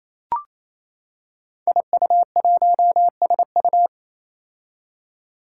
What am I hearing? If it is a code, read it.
Morse code, heard as a keyed tone, about 28 wpm, 700 Hz. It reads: IU1SU
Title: Five Challenging Letter-Number Combinations 28wpm